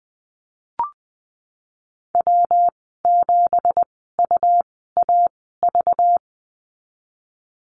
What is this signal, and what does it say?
Morse code, heard as a keyed tone, about 19 wpm, spaced as for 14 wpm, 700 Hz. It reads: W7UAV
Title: 193 Lesson 31g - Letter v - Call Signs 20wpm